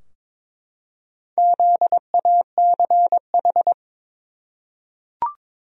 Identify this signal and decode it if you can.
Morse code, heard as a keyed tone, about 22 wpm, 700 Hz. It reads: ZACH
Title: USA Common Names - 22wpm